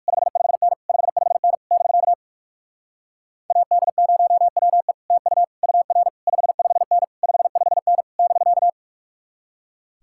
Morse code, heard as a keyed tone, about 57 wpm, 700 Hz. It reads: 55N 55N <BK> AD0WE TU UR 55N 55N <BK>